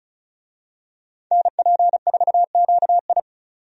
Morse code, heard as a keyed tone, about 36 wpm, 700 Hz. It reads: NP4QI
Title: USA Call Signs - 5 Characters - 35wpm